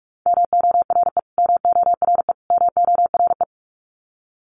Morse code, heard as a keyed tone, about 44 wpm, 700 Hz. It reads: MORE MORE MORE
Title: Top 200 Words Repeated 45wpm